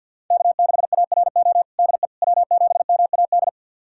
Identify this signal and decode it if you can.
Morse code, heard as a keyed tone, about 53 wpm, 690 Hz. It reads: K6ARO DE W7MAD
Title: USA Callsign DE Callsign - 5 Characters - 50wpm